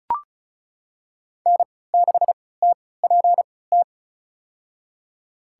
Morse code, heard as a keyed tone, about 34 wpm, spaced as for 20 wpm, 700 Hz. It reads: N6TPT
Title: USA Call Signs - 5 Characters - 3x Character Spacing - 35wpm